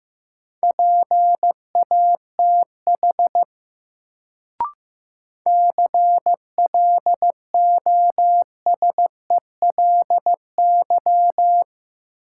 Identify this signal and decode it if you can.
Morse code, heard as a keyed tone, about 15 wpm, 700 Hz. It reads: PATH CLOSELY
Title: Sentences from Top 1400 Words - New Word Drill - 15wpm